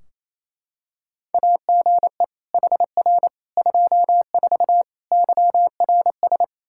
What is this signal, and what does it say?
Morse code, heard as a keyed tone, about 27 wpm, 700 Hz. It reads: AGE HR 24 YRS
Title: K1USN SST QSO Practice 28wpm